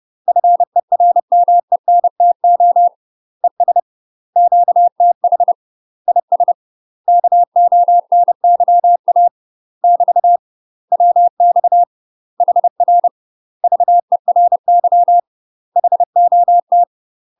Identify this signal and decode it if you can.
Morse code, heard as a keyed tone, about 30 wpm, 700 Hz. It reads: FERMENTO ES QTH IS KONYA <BT> WX HR VERY HOT
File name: UK MASTER QSOs TRAINING / 30WPM / UK MASTER QSOs 0017